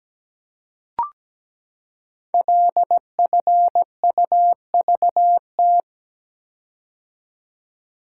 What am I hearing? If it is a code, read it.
Morse code, heard as a keyed tone, about 17 wpm, 700 Hz. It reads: LFUVT
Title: Code Groups - 5 Letters - 17wpm